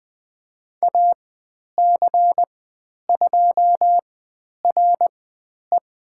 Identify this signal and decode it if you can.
Morse code, heard as a keyed tone, about 20 wpm, spaced as for 10 wpm, 700 Hz. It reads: AC2RE
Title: 102 Lesson 17g - Letter u - Call Signs 20wpm